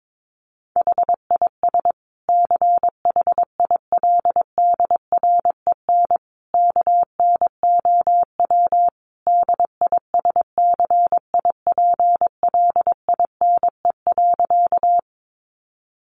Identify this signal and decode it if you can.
Morse code, heard as a keyed tone, about 22 wpm, 700 Hz. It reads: HIS CHILDREN KNOW DISCIPLINE.
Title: Sentences from Top 900 Words 22wpm